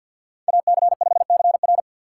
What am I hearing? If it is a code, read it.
Morse code, heard as a keyed tone, about 51 wpm, 700 Hz. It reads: AC5CR
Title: USA Call Signs - 5 Characters - 3x Character Spacing - 50wpm